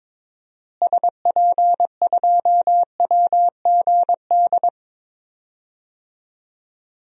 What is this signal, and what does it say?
Morse code, heard as a keyed tone, about 22 wpm, 700 Hz. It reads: SP2WGD
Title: DXCC-100 Call Signs - 6 Characters - Repeated 2x - 22wpm